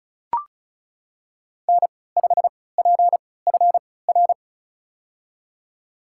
Morse code, heard as a keyed tone, about 34 wpm, spaced as for 20 wpm, 700 Hz. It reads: N5PFR
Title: USA Call Signs - 5 Characters - 3x Character Spacing - 35wpm